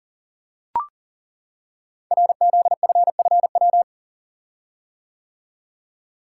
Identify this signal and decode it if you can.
Morse code, heard as a keyed tone, about 40 wpm, 700 Hz. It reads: RZFFW
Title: Code Groups - 5 Letters - 40wpm